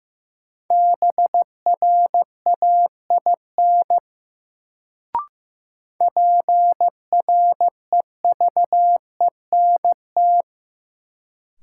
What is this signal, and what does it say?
Morse code, heard as a keyed tone, about 15 wpm, 700 Hz. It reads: BRAIN PREVENT
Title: Sentences from Top 1100 Words - New Word Drill Alt - 15wpm